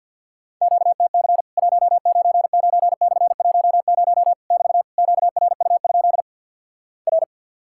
Morse code, heard as a keyed tone, about 50 wpm, 700 Hz. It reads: QTC 199/10 <BT> QRU? R
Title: Worked All Europe DX Contest - QTC Exchange with Repeats - 50wpm